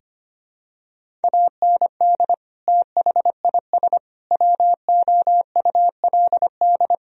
Morse code, heard as a keyed tone, about 25 wpm, 700 Hz. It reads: AND THIS WOULD
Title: Sentences from Top 1000 Words - Review Part 1 - 25wpm